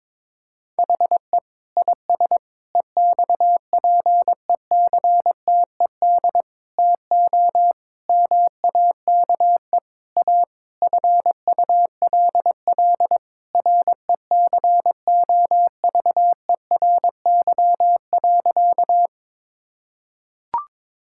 Morse code, heard as a keyed tone, about 22 wpm, 700 Hz. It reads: HE IS EXPECTED TO MAKE A FULL RECOVERY.
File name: Sentences with 7-Word Phrases - 22wpm